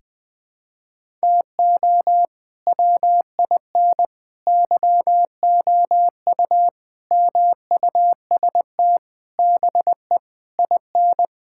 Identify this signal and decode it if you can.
Morse code, heard as a keyed tone, about 20 wpm, 700 Hz. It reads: TO WIN YOU MUST BE IN
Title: Sentences with 9-Word Phrases - 20wpm